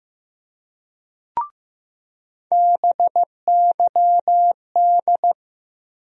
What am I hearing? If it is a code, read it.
Morse code, heard as a keyed tone, about 15 wpm, 700 Hz. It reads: BYD
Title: USA Call Sign Postfixes - 3 Letters 15wpm